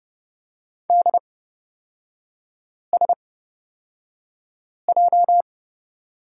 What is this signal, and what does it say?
Morse code, heard as a keyed tone, about 30 wpm, spaced as for 5 wpm, 700 Hz. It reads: DSJ
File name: Lesson 38e - Letter q - Three Characters 30wpm